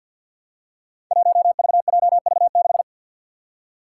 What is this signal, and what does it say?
Morse code, heard as a keyed tone, about 50 wpm, 700 Hz. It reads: 1VJV6
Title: Five Challenging Letter-Number Combinations 50wpm